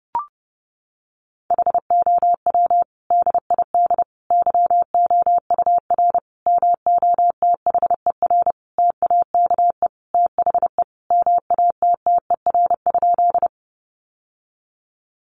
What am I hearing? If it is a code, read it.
Morse code, heard as a keyed tone, about 30 wpm, 700 Hz. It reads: HOW DID YOUR MOTHER TAKE THE MATTER?
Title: Sentences from Top 700 Words 30wpm